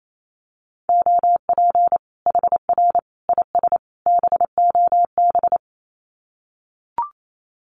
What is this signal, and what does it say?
Morse code, heard as a keyed tone, about 28 wpm, 700 Hz. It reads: OP HR IS BOB
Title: Common QSO Phrases 28wpm